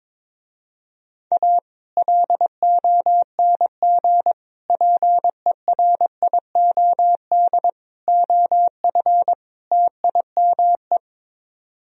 Morse code, heard as a keyed tone, about 23 wpm, 700 Hz. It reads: A LONG PERIOD OF TIME
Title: Speed-Racing - Sets of 5 Words 22wpm